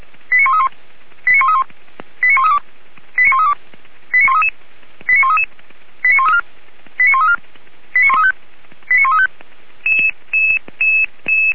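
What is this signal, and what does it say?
Five-tone selective call (ZVEI), heard as quick runs of five tones, tones of about 70 ms. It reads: 89131 89131 89132 89132 89133 89133 89134 89134 89135 89135
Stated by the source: ZVEI five-tone selective call received by SDR.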